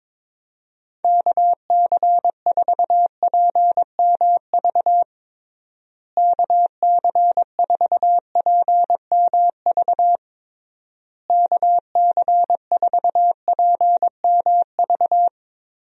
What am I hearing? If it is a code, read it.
Morse code, heard as a keyed tone, about 22 wpm, 700 Hz. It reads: KC4PMV KC4PMV KC4PMV
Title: USA Call Signs - 6 Characters - Repeated 3x - 22wpm